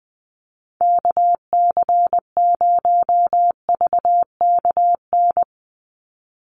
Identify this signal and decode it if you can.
Morse code, heard as a keyed tone, about 20 wpm, 700 Hz. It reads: KC0VKN
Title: Call Signs Repeated 2x - 20wpm